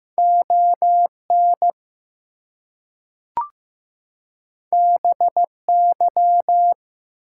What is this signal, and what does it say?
Morse code, heard as a keyed tone, about 15 wpm, 700 Hz. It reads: ON BY
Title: Two Letter Words - 15wpm